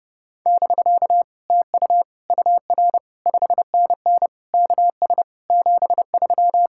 Morse code, heard as a keyed tone, about 29 wpm, 700 Hz. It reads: <BK> TU UR 5NN KS 73